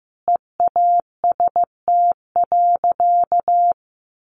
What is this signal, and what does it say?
Morse code, heard as a keyed tone, about 15 wpm, 700 Hz. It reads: EAST.
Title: Sentences from Top 1400 Words 15wpm